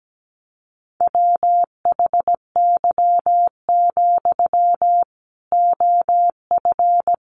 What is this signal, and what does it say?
Morse code, heard as a keyed tone, about 17 wpm, 700 Hz. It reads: WHY, OF
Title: Sentences from Top 500 Words 17wpm